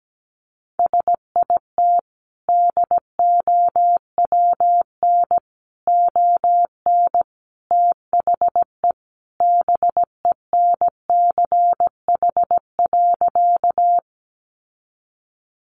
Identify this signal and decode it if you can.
Morse code, heard as a keyed tone, about 17 wpm, 700 Hz. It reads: SIT DOWN ON THE BENCH.
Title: Sentences from Top 1400 Words 17wpm